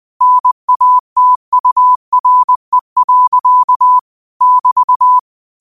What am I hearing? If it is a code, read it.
Morse code, heard as a keyed tone, about 20 wpm, 1000 Hz. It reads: NATURE. <BT>
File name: qotdfiles / 20wpm